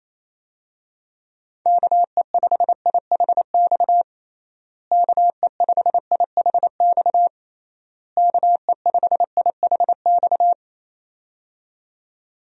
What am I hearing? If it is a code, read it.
Morse code, heard as a keyed tone, about 28 wpm, 700 Hz. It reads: KE5IHX KE5IHX KE5IHX
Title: USA Call Signs - 6 Characters - Repeated 3x - 28wpm